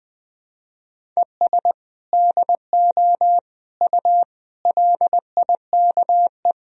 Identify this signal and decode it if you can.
Morse code, heard as a keyed tone, about 20 wpm, 700 Hz. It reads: ES DO U LIKE
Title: Sentences from Top 100 Words and Abbreviations 20wpm